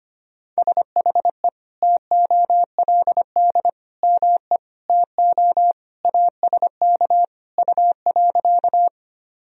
Morse code, heard as a keyed tone, about 25 wpm, 700 Hz. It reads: SHE TOLD ME TO ASK U.
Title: Sentences from Top 400 Words and Abbreviations 25wpm